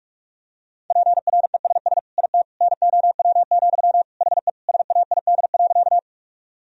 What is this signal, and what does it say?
Morse code, heard as a keyed tone, about 46 wpm, 700 Hz. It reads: PRESS IT NOW, HE SAID.
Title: Sentences from Top 1000 Words - Review Part 1 - 45wpm